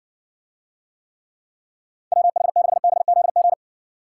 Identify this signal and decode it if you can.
Morse code, heard as a keyed tone, about 60 wpm, 700 Hz. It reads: WH6BZG